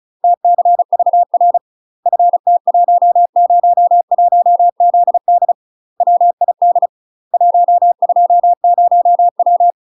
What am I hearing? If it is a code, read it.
Morse code, heard as a keyed tone, about 35 wpm, 700 Hz. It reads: TCVR FT101ZD WID 120W
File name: UK MASTER QSOs TRAINING / 35WPM / UK MASTER QSOs 0016